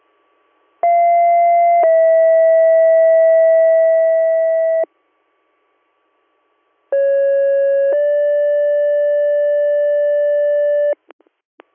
Two-tone page, heard as a one-second tone, then a three-second tone, about 690 then 650 Hz, about 550 then 580 Hz. It is heard from 0.8 to 10.9 s.